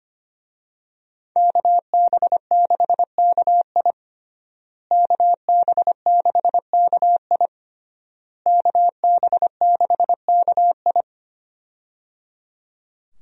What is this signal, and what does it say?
Morse code, heard as a keyed tone, about 25 wpm, 700 Hz. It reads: KB6KI KB6KI KB6KI
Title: USA Call Signs - 5 Characters - Repeated 3x - 25wpm